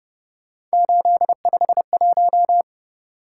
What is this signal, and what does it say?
Morse code, heard as a keyed tone, about 29 wpm, 700 Hz. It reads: 851